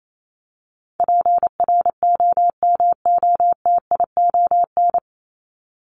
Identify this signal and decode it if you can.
Morse code, heard as a keyed tone, about 28 wpm, 700 Hz. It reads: PROMOTION
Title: Top 1501-1750 Words 28wpm